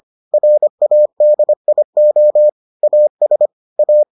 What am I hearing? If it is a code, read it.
Morse code, heard as a keyed tone, about 25 wpm, 600 Hz. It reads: RADIO AS A